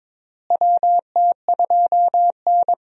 Morse code, heard as a keyed tone, about 22 wpm, 700 Hz. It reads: WT2N
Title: USA Call Signs - 4 Characters - Repeated 2x - 22wpm